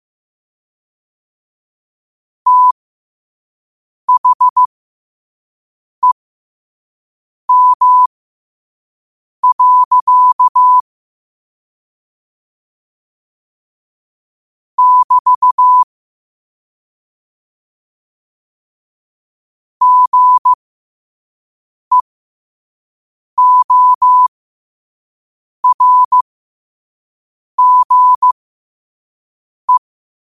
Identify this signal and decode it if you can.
Morse code, heard as a keyed tone, about 15 wpm, spaced as for 5 wpm, 1000 Hz. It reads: THEM. <BT> GEORGE
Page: Quotes for Fri, 15 Aug 2025 in Morse Code at 5 words per minute.